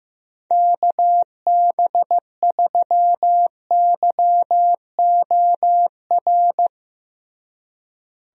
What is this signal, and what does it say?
Morse code, heard as a keyed tone, about 15 wpm, 700 Hz. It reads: KB3YOR
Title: USA Call Signs - 6 Characters - Repeated 2x - 15wpm